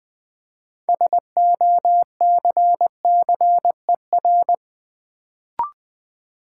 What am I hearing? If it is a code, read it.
Morse code, heard as a keyed tone, about 20 wpm, 700 Hz. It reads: SOCCER
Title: Build A Word - 3 Letters At A Time - 6-Letter Words - 20wpm